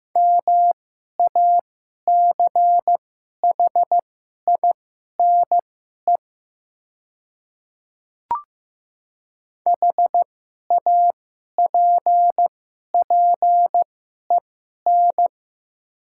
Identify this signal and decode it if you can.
Morse code, heard as a keyed tone, about 15 wpm, spaced as for 11 wpm, 700 Hz. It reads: MACHINE HAPPEN
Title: 2x Character Spacing - Top 500 Words - 15wpm